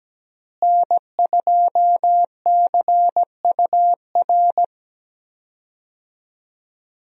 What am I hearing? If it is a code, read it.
Morse code, heard as a keyed tone, about 17 wpm, 700 Hz. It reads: N2CUR
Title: USA Call Signs - 5 Characters - Repeated 3x - 17wpm